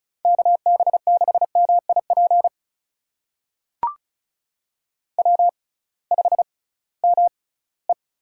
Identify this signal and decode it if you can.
Morse code, heard as a keyed tone, about 36 wpm, spaced as for 20 wpm, 700 Hz. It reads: KB6MIP W5ME
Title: USA Call Signs - 6x Character Spacing - 35wpm